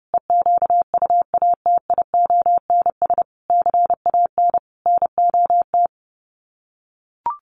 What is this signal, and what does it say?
Morse code, heard as a keyed tone, about 30 wpm, 700 Hz. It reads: EQUATIONS CAN NOT
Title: Sets of 5 Words 30wpm